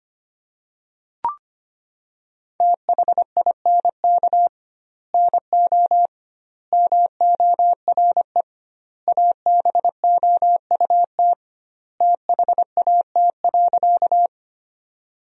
Morse code, heard as a keyed tone, about 25 wpm, 700 Hz. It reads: THINK NO MORE ABOUT THAT.
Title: 2x Word-Spacing - Sentences from Top 200 Words - 25wpm